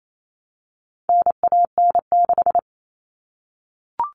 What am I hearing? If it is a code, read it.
Morse code, heard as a keyed tone, about 28 wpm, 700 Hz. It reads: NAN6
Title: All Cut Numbers - 4-Digit 28wpm